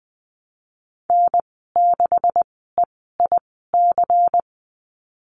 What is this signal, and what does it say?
Morse code, heard as a keyed tone, about 20 wpm, spaced as for 14 wpm, 700 Hz. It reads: N6EIC
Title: 128 Lesson 21g - Number 6 - Call Signs 20wpm